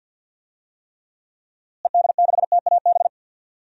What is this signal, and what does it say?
Morse code, heard as a keyed tone, about 52 wpm, 700 Hz. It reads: ED6TAB
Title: DXCC-100 Call Signs - 6 Characters - Repeated 3x - 50wpm